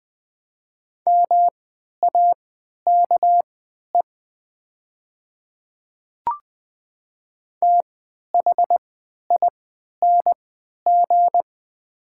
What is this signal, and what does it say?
Morse code, heard as a keyed tone, about 20 wpm, spaced as for 11 wpm, 700 Hz. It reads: MAKE THING
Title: 3x Character Spacing - Top 100 Words - 20wpm